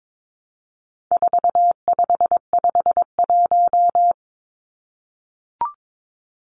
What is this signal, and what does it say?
Morse code, heard as a keyed tone, about 22 wpm, 700 Hz. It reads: 4551
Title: Numbers - 4-Digit 22wpm